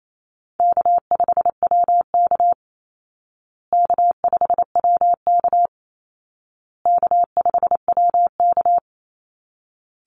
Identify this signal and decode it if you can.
Morse code, heard as a keyed tone, about 28 wpm, 700 Hz. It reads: K5WK K5WK K5WK
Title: Easy Call Signs Repeated 3x - 28wpm